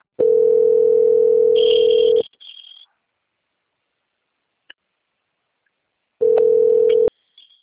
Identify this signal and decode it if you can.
Ringback tone (North America), 440 + 480 Hz.
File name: out- - - - - .8621